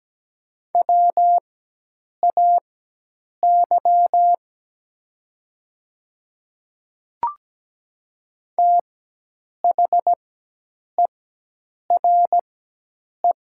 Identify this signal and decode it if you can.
Morse code, heard as a keyed tone, about 17 wpm, spaced as for 8 wpm, 700 Hz. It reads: WAY THERE
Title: 4x Character Spacing - Top 300 Words - 17wpm